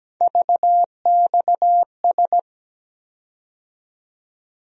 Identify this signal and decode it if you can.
Morse code, heard as a keyed tone, about 17 wpm, 700 Hz. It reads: VXS
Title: USA Call Sign Postfixes - 3 Letters 17wpm